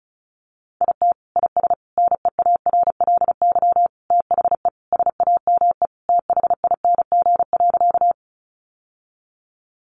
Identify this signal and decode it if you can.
Morse code, heard as a keyed tone, about 34 wpm, 700 Hz. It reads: IT IS NEARLY THE SAME THING.
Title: Sentences from Top 1300 Words 35wpm